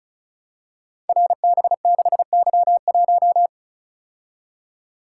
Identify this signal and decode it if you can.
Morse code, heard as a keyed tone, about 36 wpm, 700 Hz. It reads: RB6Y1
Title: Code Groups - 5 Letter:Numbers - 35wpm